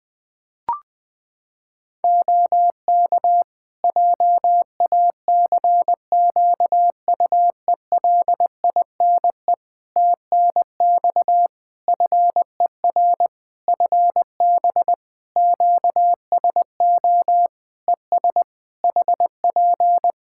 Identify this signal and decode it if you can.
Morse code, heard as a keyed tone, about 20 wpm, 700 Hz. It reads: OK JACQUELINE TNX FER FB QSO ES HP
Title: K1USN SST QSO Practice 20wpm